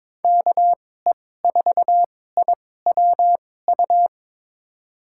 Morse code, heard as a keyed tone, about 22 wpm, spaced as for 16 wpm, 700 Hz. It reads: KE4IWU
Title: USA Call Signs - 6 Characters - 2x Character Spacing - 22wpm